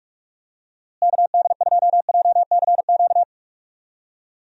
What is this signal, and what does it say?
Morse code, heard as a keyed tone, about 44 wpm, 700 Hz. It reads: KD2JCQ
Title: USA Call Signs - 6 Characters - Repeated 3x - 45wpm